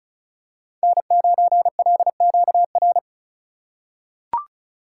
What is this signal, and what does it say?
Morse code, heard as a keyed tone, about 34 wpm, 700 Hz. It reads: N9LQR
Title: USA Call Signs - 5 Characters - 3x Character Spacing - 35wpm